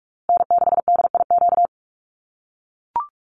Morse code, heard as a keyed tone, about 44 wpm, 700 Hz. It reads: N6DIQ
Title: USA Call Signs Repeated 3x 45wpm